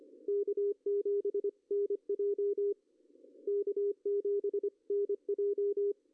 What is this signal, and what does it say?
Morse code, heard as a keyed tone, about 24 wpm, 410 Hz. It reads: K7NJ K7NJ
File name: CW / cqww-cw-k5tr